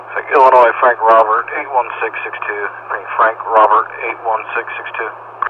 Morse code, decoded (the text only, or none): none